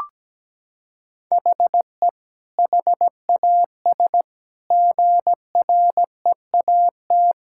Morse code, heard as a keyed tone, about 17 wpm, 700 Hz. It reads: HE HAS GREAT
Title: Sentences from Top 1000 Words - Review Part 6 - 17wpm